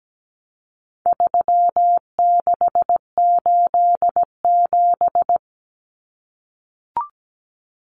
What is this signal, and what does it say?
Morse code, heard as a keyed tone, about 17 wpm, 700 Hz. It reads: 3687
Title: Numbers - 4-Digit 17wpm